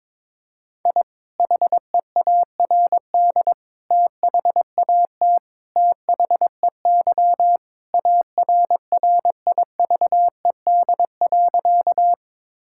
Morse code, heard as a keyed tone, about 22 wpm, 700 Hz. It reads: I HEARD THAT THEY ARRIVED.
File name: practice / 22wpm / Sentences with 2-Word Phrases - 22wpm